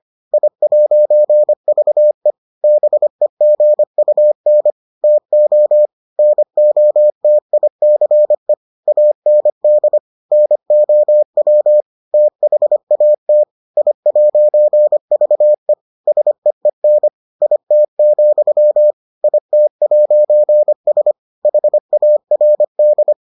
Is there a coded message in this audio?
Yes: Morse code from 0.3 to 23.1 s, heard as a keyed tone, about 25 wpm, 600 Hz. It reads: I'VE BEGUN TO NOTICE AND NOW THAT I'VE SEEN IT, IT'S HARD